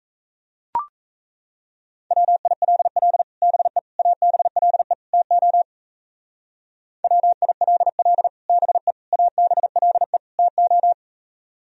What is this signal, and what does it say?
Morse code, heard as a keyed tone, about 34 wpm, 700 Hz. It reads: WILL BE ABLE TO WILL BE ABLE TO
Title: Speed-Racing - Sets of 4 Words 28wpm